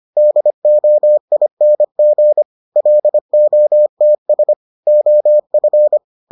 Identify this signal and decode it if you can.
Morse code, heard as a keyed tone, about 25 wpm, 600 Hz. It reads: DOING LOTS OF